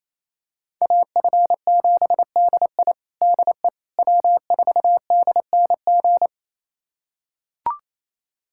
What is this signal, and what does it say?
Morse code, heard as a keyed tone, about 28 wpm, 700 Hz. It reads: AF7DI DE W4DNG
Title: USA Callsign DE Callsign - 5 Characters - 28wpm